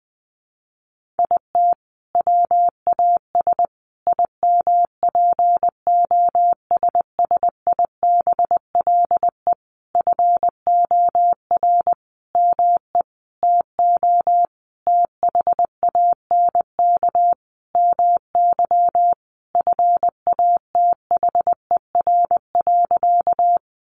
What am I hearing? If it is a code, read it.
Morse code, heard as a keyed tone, about 20 wpm, 700 Hz. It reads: IT WAS IMPOSSIBLE FOR ME TO THANK MY FATHER.
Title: Sentences from Top 1100 Words 20wpm